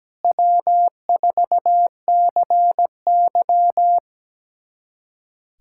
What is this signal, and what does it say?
Morse code, heard as a keyed tone, about 17 wpm, 700 Hz. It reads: W4CY
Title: USA Callsign DE Callsign - 4 Characters - 17wpm